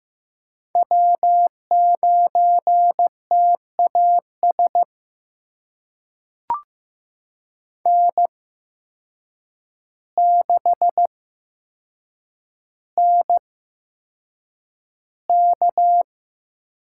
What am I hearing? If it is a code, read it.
Morse code, heard as a keyed tone, about 15 wpm, spaced as for 7 wpm, 700 Hz. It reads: W9TAS N6NK
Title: USA Call Signs - 8x Character Spacing - 15wpm